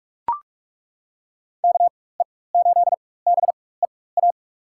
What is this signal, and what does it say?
Morse code, heard as a keyed tone, about 47 wpm, spaced as for 21 wpm, 700 Hz. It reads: KE8BEA
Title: USA Call Signs - 6 Characters - 4x Character Spacing - 45wpm